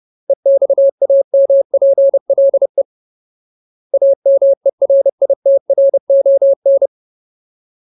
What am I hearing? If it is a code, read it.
Morse code, heard as a keyed tone, about 30 wpm, 550 Hz. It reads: EXAMPLE AMERITRON
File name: Ultimate Word List / 30WPM Ultimate